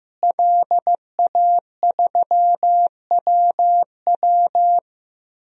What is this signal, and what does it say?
Morse code, heard as a keyed tone, about 15 wpm, 700 Hz. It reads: LA3WW
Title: DXCC-100 Call Signs - 5 Characters - Repeated 2x - 15wpm